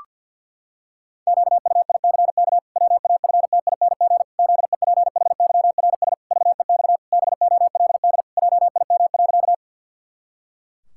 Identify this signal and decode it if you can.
Morse code, heard as a keyed tone, about 51 wpm, 700 Hz. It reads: QUICK WAFTING ZEPHYRS VEX BOLD JIM.
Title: Pangram Challenge v2 - 50wpm